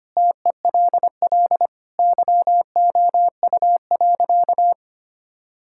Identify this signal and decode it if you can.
Morse code, heard as a keyed tone, about 25 wpm, 700 Hz. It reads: TELL YOU.
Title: Sentences from Top 1000 Words - Review Part 5 - 25wpm